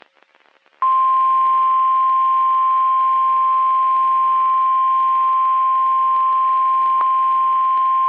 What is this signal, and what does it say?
Emergency Alert System (SAME) tones (the 1050 Hz attention tone). It is heard from 0.8 to 7.0 s.
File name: eas messages / EAS WXM31 Elmira